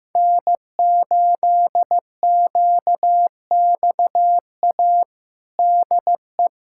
Morse code, heard as a keyed tone, about 15 wpm, 700 Hz. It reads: N8QXA DE
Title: USA Callsign DE Callsign - 5 Characters - 15wpm